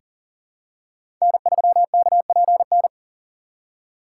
Morse code, heard as a keyed tone, about 40 wpm, 700 Hz. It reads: N3KPN
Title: USA Call Signs - 5 Characters - 3x Character Spacing - 40wpm